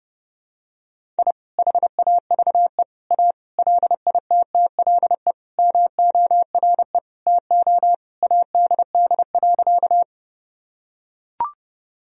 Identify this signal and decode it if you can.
Morse code, heard as a keyed tone, about 30 wpm, 700 Hz. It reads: I HAVE A LITTLE MORE TO ADD.
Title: Sentences from Top 1000 Words - Review Part 1 - 30wpm